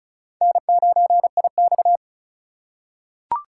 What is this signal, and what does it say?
Morse code, heard as a keyed tone, about 35 wpm, 700 Hz. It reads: N9IX
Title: USA Call Signs - 3x Character Spacing - 35wpm